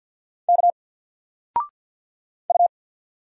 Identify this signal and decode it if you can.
Morse code, heard as a keyed tone, about 50 wpm, 700 Hz. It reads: K U
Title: Single Letter-Number 50wpm